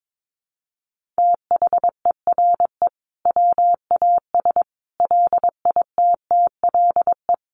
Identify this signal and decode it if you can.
Morse code, heard as a keyed tone, about 22 wpm, 700 Hz. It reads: THERE WAS LITTLE
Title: Sentences from Top 1400 Words 22wpm